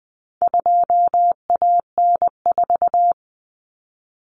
Morse code, heard as a keyed tone, about 20 wpm, 700 Hz. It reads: 2AN4
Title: Common Cut Numbers - 4-Digit 20wpm